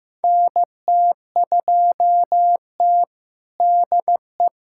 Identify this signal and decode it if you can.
Morse code, heard as a keyed tone, about 15 wpm, 700 Hz. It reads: NT2T DE
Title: USA Callsign DE Callsign - 4 Characters - 15wpm